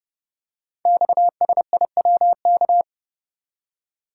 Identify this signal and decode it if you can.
Morse code, heard as a keyed tone, about 30 wpm, 700 Hz. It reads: XSIWK
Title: Code Groups - 5 Letters - 30wpm